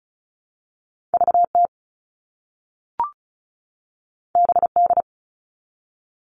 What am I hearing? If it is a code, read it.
Morse code, heard as a keyed tone, about 36 wpm, 700 Hz. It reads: VT BD
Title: All Cut Numbers - 2-Digit 35wpm